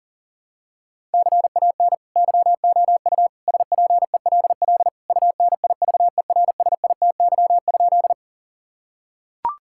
Morse code, heard as a keyed tone, about 40 wpm, 700 Hz. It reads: CAN YOU SPELL UNIVERSITY?
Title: Sentences from Top 1000 Words - Review Part 5 - 40wpm